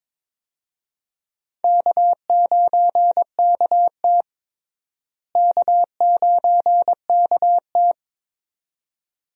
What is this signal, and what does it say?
Morse code, heard as a keyed tone, about 21 wpm, 700 Hz. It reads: K9KT K9KT
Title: USA Call Signs - 4 Characters - Repeated 2x - 22wpm